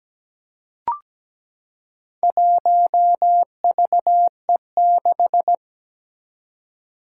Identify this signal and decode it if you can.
Morse code, heard as a keyed tone, about 17 wpm, 700 Hz. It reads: 1VE6
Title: Four Challenging Letter-Number Combinations 17wpm